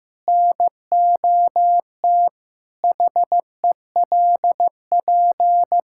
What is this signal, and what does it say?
Morse code, heard as a keyed tone, about 15 wpm, 700 Hz. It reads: NOT HELP
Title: Sentences from Top 1600 Words 15wpm